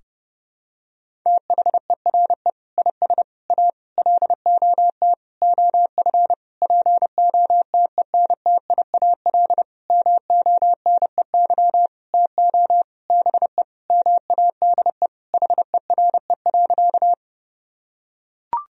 Morse code, heard as a keyed tone, about 32 wpm, 700 Hz. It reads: THERE IS A LOT OF POTENTIAL MONEY TO BE MADE HERE.
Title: Sentences from Top 1000 Words - Review Part 4 - 30wpm